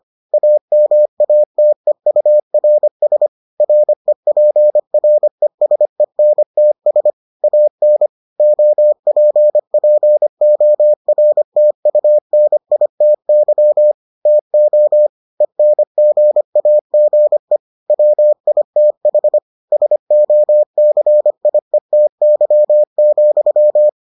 Morse code, heard as a keyed tone, about 25 wpm, 600 Hz. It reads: AMATEURS REPRESENTS AN OPPORTUNITY TO ENGAGE WITH SOCIETY,